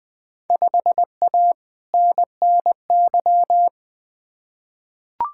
Morse code, heard as a keyed tone, about 20 wpm, 700 Hz. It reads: 5A NNY
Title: Field Day Exchanges - 20wpm